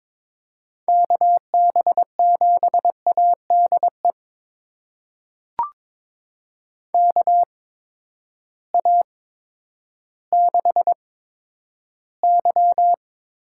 Morse code, heard as a keyed tone, about 22 wpm, spaced as for 11 wpm, 700 Hz. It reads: KB7ADE KA6Y
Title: USA Call Signs - 6 Characters - 8x Character Spacing - 22wpm